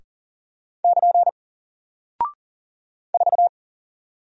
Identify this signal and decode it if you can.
Morse code, heard as a keyed tone, about 40 wpm, 700 Hz. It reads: <KN> 4
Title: Single Character (Letter:Number:Punctuation:Prosign) Alt 40wpm